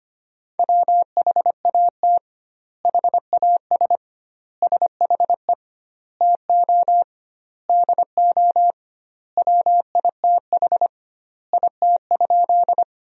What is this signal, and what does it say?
Morse code, heard as a keyed tone, about 25 wpm, 700 Hz. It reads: WHAT HAS SHE TO DO WITH IT?
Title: 2x Word-Spacing - Sentences from Top 100 Words - 25wpm